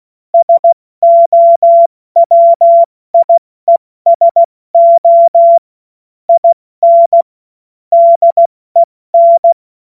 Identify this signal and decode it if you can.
Morse code, heard as a keyed tone, about 16 wpm, 680 Hz. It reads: SOWIESO IN DEN
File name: Morsekurs / Witze / Witz 16. Februar 2021, 70 F80